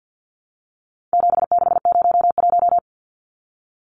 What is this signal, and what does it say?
Morse code, heard as a keyed tone, about 50 wpm, 700 Hz. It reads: 7601
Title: Numbers - 4-Digit 50wpm